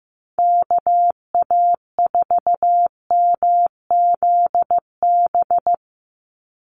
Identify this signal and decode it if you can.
Morse code, heard as a keyed tone, about 15 wpm, 700 Hz. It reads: KA4MZB
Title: USA Call Signs 15wpm